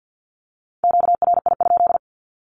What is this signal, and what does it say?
Morse code, heard as a keyed tone, about 50 wpm, 700 Hz. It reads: QRI?
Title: Q-Codes Repeated 3x 50wpm